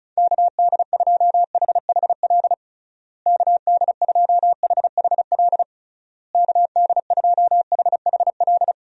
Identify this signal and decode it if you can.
Morse code, heard as a keyed tone, about 36 wpm, 700 Hz. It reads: KD2HHL KD2HHL KD2HHL
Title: USA Call Signs - 6 Characters - Repeated 3x - 35wpm